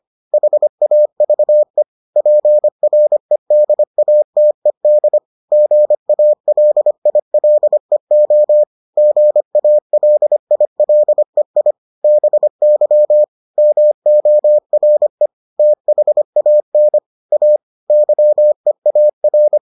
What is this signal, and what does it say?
Morse code, heard as a keyed tone, about 25 wpm, 600 Hz. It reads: HAVE PREDATED GALILEO GALILEI BY MORE THAN A YEAR